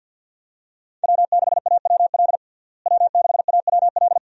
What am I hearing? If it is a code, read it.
Morse code, heard as a keyed tone, about 51 wpm, 700 Hz. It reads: W6AWL W6AWL
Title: USA Call Signs - 5 Characters - Repeated 2x - 50wpm